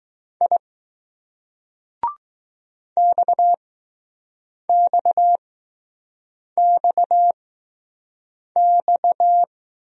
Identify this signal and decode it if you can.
Morse code, heard as a keyed tone, about 20 wpm, spaced as for 6 wpm, 700 Hz. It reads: I XXXX